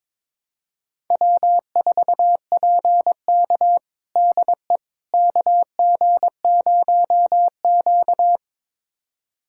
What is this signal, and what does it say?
Morse code, heard as a keyed tone, about 22 wpm, 700 Hz. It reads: W4PK DE KG0Q
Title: USA Callsign DE Callsign - 4 Characters - 22wpm